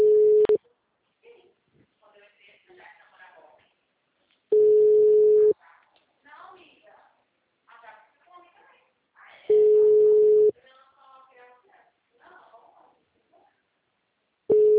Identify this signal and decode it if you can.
Ringback tone (Europe, 425 Hz), with speech.